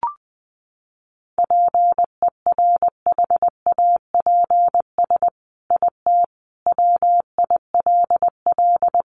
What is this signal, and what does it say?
Morse code, heard as a keyed tone, about 20 wpm, 700 Hz. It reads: PERHAPS IT WILL
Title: Sentences from Top 1300 Words 20wpm